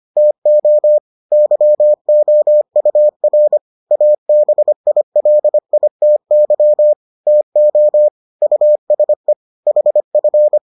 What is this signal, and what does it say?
Morse code, heard as a keyed tone, about 25 wpm, 600 Hz. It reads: TO YOUR ABILITY TO USE HF